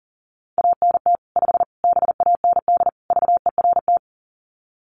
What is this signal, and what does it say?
Morse code, heard as a keyed tone, about 37 wpm, 700 Hz. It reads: ANT 5 BAND VERT
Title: Common QSO Phrases 40wpm